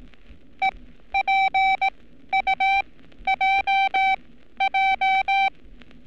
Morse code, heard as a keyed tone, about 18 wpm, spaced as for 12 wpm, 780 Hz. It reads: EPUJJ
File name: training / NAVPERS 11224-11235 / NAVPERS 11227 RA RB